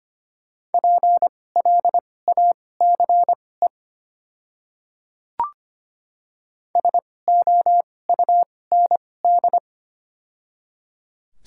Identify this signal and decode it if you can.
Morse code, heard as a keyed tone, about 25 wpm, spaced as for 18 wpm, 700 Hz. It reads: PLACE SOUND
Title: 2x Character Spacing - Top 200 Words - 25wpm